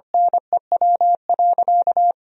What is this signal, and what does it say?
Morse code, heard as a keyed tone, about 25 wpm, 700 Hz. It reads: NEW.
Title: 4x Word-Spacing - Sentences from Top 400 Words - 25wpm